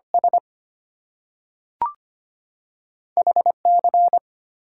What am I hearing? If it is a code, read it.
Morse code, heard as a keyed tone, about 25 wpm, 700 Hz. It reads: S HC
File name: Two Letter-Number Combinations - 25wpm